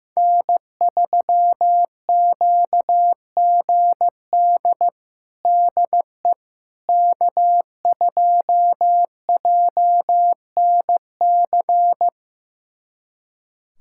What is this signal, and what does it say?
Morse code, heard as a keyed tone, about 15 wpm, 700 Hz. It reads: N3QGD DE K2JNC
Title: USA Callsign DE Callsign - 5 Characters - 15wpm